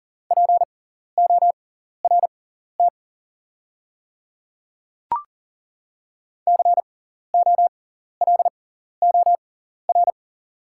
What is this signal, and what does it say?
Morse code, heard as a keyed tone, about 40 wpm, spaced as for 14 wpm, 700 Hz. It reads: PORT COLOR
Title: 6x Character Spacing - Top 500 Words - 40wpm